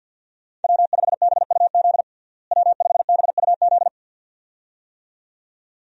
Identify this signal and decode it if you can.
Morse code, heard as a keyed tone, about 51 wpm, 700 Hz. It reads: W5BUZ W5BUZ
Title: USA Call Signs - 5 Characters - Repeated 2x - 50wpm